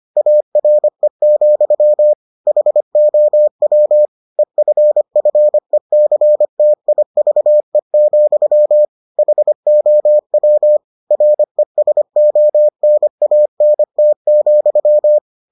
Morse code, heard as a keyed tone, about 25 wpm, 600 Hz. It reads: ARE, HOW EFFECTIVE, HOW RESONANT,